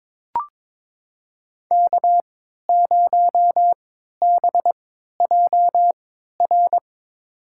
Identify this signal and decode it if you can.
Morse code, heard as a keyed tone, about 22 wpm, spaced as for 12 wpm, 700 Hz. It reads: K0BJR
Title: USA Call Signs - 5 Characters - 3x Character Spacing - 22wpm